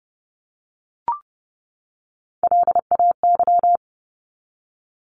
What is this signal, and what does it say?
Morse code, heard as a keyed tone, about 30 wpm, 700 Hz. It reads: LAY
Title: Top 500 Words 30wpm